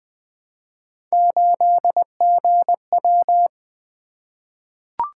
Morse code, heard as a keyed tone, about 20 wpm, 700 Hz. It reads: 8GW
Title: Three Letter-Number Combinations - 20wpm